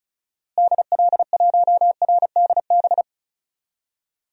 Morse code, heard as a keyed tone, about 34 wpm, 700 Hz. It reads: DL1RDB
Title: DXCC-100 Call Signs - 6 Characters - Repeated 3x - 35wpm